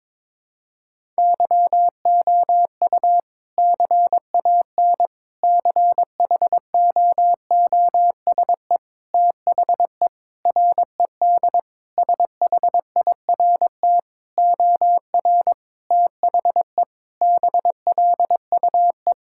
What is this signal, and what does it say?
Morse code, heard as a keyed tone, about 22 wpm, 700 Hz. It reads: YOU CAN CHOOSE THE RED SHIRT OR THE BLUE
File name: practice / 22wpm / Sentences with 2-Word Phrases - 22wpm